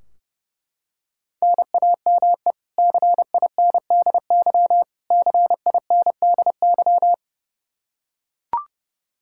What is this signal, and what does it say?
Morse code, heard as a keyed tone, about 30 wpm, 700 Hz. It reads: NAME CINDY CINDY
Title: K1USN SST QSO Practice 30wpm